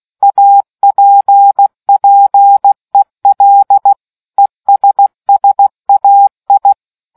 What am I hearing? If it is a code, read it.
Morse code, heard as a keyed tone, about 16 wpm, 800 Hz. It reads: APPEL ESSAI